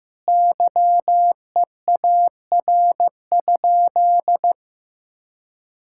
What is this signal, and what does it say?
Morse code, heard as a keyed tone, about 15 wpm, 700 Hz. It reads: YEAR?
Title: Sentences from Top 1000 Words - Review Part 5 - 15wpm